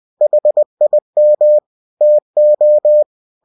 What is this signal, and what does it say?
Morse code, heard as a keyed tone, about 20 wpm, 600 Hz. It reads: HIM TO
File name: cw-books / winnie-the-pooh / winnie-20wpm-page-0103